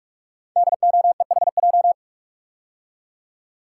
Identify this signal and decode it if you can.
Morse code, heard as a keyed tone, about 44 wpm, 690 Hz. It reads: DOEHJ
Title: Code Groups - 5 Letters - 45wpm